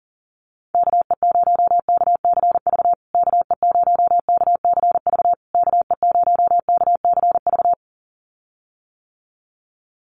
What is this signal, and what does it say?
Morse code, heard as a keyed tone, about 40 wpm, 700 Hz. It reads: KE0KCV KE0KCV KE0KCV
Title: USA Call Signs Repeated 3x 40wpm